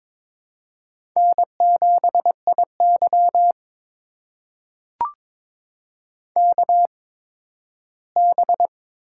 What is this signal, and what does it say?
Morse code, heard as a keyed tone, about 21 wpm, spaced as for 13 wpm, 700 Hz. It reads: N7IY KB
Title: USA Call Signs - 4 Characters - 8x Character Spacing - 22wpm